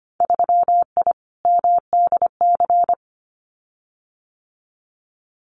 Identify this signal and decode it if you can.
Morse code, heard as a keyed tone, about 25 wpm, 690 Hz. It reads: 3I MDC
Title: Winter Field Day Exchanges Encore 25wpm